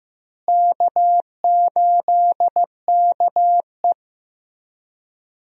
Morse code, heard as a keyed tone, about 15 wpm, 700 Hz. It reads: K8KE
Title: USA Call Signs - 4 Characters - Repeated 3x - 15wpm